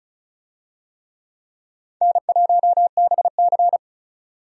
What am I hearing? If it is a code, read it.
Morse code, heard as a keyed tone, about 36 wpm, 700 Hz. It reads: N1BC
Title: USA Call Signs - 4 Characters - 35wpm